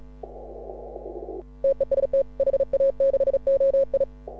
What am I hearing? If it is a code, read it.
Morse code, heard as a keyed tone, about 35 wpm, 530 Hz. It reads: TEST HA6OI